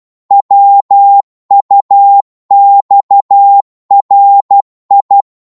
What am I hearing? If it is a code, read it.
Morse code, heard as a keyed tone, about 12 wpm, 800 Hz. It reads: WUXRI
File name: Morse / ltr-12wpm